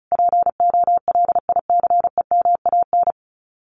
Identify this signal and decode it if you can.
Morse code, heard as a keyed tone, about 34 wpm, 700 Hz. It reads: POLICEMAN
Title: Common QSO Phrases Encore 35wpm